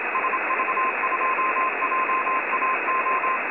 Morse code, heard as a keyed tone, about 25 wpm, 1040 Hz, with radio noise. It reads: SV2JAO
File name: websdr recording 70175.0kHz